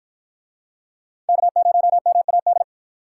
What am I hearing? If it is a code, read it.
Morse code, heard as a keyed tone, about 49 wpm, 700 Hz. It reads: K0MAD